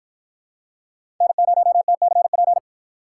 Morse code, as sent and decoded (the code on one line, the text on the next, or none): -. ----- - -.-. .--.
N0TCP